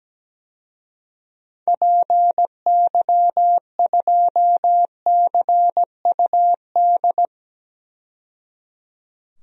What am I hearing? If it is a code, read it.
Morse code, heard as a keyed tone, about 17 wpm, 700 Hz. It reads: PY2CUD
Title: DXCC-100 Call Signs - 6 Characters - 17wpm